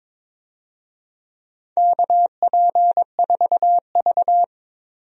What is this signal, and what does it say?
Morse code, heard as a keyed tone, about 22 wpm, 700 Hz. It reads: KP4V
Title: DXCC-100 Call Signs - 4 Characters - 22wpm